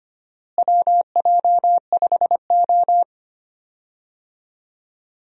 Morse code, heard as a keyed tone, about 25 wpm, 700 Hz. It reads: WJ5O
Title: USA Call Signs - 4 Characters - Repeated 3x - 25wpm